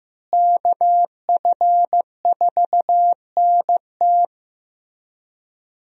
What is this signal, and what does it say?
Morse code, heard as a keyed tone, about 15 wpm, 700 Hz. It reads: KF4NT
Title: USA Call Signs - 5 Characters - Repeated 3x - 15wpm